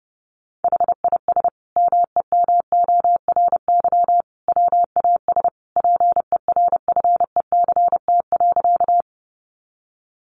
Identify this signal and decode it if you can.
Morse code, heard as a keyed tone, about 30 wpm, 700 Hz. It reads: HIS MEMORY WAS PERFECT.
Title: Sentences from Top 1300 Words 30wpm